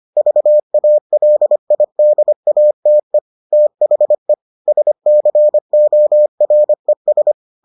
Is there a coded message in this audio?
Yes: Morse code from 0.2 to 7.3 s, heard as a keyed tone, about 25 wpm, 600 Hz. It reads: VALIDATE THE SCORES